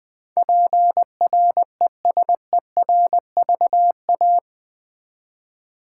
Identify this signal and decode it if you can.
Morse code, heard as a keyed tone, about 20 wpm, 700 Hz. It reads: PRESERVA
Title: Build A Word - 4 Letters At A Time - 12-Letter Words - 20wpm